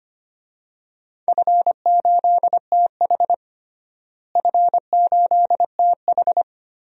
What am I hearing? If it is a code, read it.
Morse code, heard as a keyed tone, about 25 wpm, 700 Hz. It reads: F8TH F8TH
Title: DXCC-100 Call Signs - 4 Characters - Repeated 2x - 25wpm